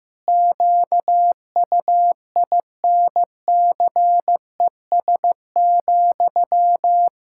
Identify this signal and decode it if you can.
Morse code, heard as a keyed tone, about 15 wpm, 700 Hz. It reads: QUINCES,
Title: Pangram Challenge v2 - 15wpm